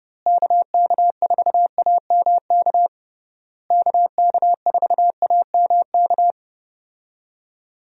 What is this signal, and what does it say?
Morse code, heard as a keyed tone, about 31 wpm, 700 Hz. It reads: KK4AMK KK4AMK
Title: USA Call Signs - 6 Characters - Repeated 3x - 30wpm